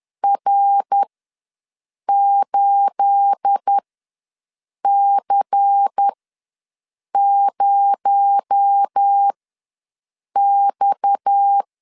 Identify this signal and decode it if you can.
Morse code, heard as a keyed tone, about 10 wpm, spaced as for 6 wpm, 800 Hz. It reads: R8C0X
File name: LEZIONI / 9 28 ws 10wpm 4124